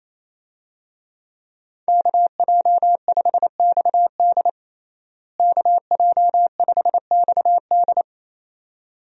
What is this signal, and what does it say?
Morse code, heard as a keyed tone, about 28 wpm, 700 Hz. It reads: KJ5XD KJ5XD
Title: USA Call Signs - 5 Characters - Repeated 2x - 28wpm